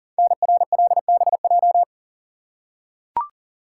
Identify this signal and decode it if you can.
Morse code, heard as a keyed tone, about 39 wpm, 700 Hz. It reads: NRLBJ
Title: Code Groups - 5 Letters - 40wpm